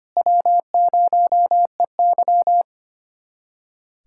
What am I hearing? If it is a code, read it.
Morse code, heard as a keyed tone, about 25 wpm, 700 Hz. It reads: W0EY
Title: USA Callsign DE Callsign - 4 Characters - 25wpm